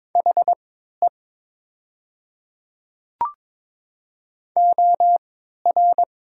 Morse code, heard as a keyed tone, about 22 wpm, spaced as for 12 wpm, 700 Hz. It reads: HE OR
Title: 3x Character Spacing - Top 200 Words - 22wpm